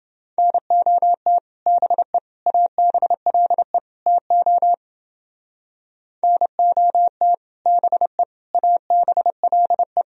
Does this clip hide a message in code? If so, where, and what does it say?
Yes: Morse code from 0.4 to 10.0 s, heard as a keyed tone, about 29 wpm, 700 Hz. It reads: NOT BE ABLE TO NOT BE ABLE